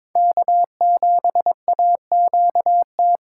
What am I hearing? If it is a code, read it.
Morse code, heard as a keyed tone, about 22 wpm, 700 Hz. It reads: K7AQT
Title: USA Call Signs - 5 Characters - 4x Character Spacing - 22wpm